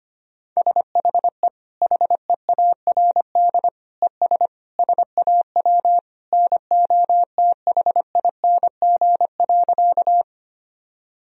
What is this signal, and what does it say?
Morse code, heard as a keyed tone, about 25 wpm, 700 Hz. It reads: SHE HEARD ES SAW NOTHING.
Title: Sentences from Top 500 Words and Abbreviations 25wpm